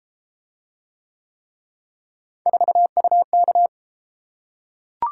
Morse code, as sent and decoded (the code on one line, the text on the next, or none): ....- ..- -.-
4UK